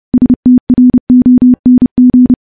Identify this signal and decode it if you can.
Morse code, heard as a keyed tone, about 31 wpm, 250 Hz. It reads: STRONG
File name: morse / morse-training-30WPM